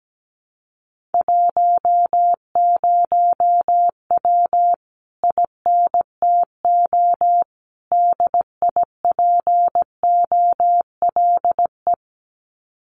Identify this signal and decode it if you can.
Morse code, heard as a keyed tone, about 17 wpm, 700 Hz. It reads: 10W INTO DIPOLE
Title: Common QSO Phrases Encore 17wpm